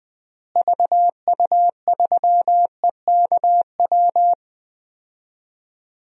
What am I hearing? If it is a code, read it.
Morse code, heard as a keyed tone, about 20 wpm, 700 Hz. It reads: VU3EKW
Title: DXCC-100 Call Signs - 6 Characters - Repeated 3x - 20wpm